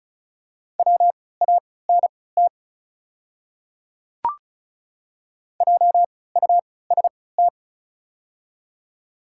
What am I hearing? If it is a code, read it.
Morse code, heard as a keyed tone, about 34 wpm, spaced as for 20 wpm, 700 Hz. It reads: WANT JUST
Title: 3x Character Spacing - Top 300 Words - 35wpm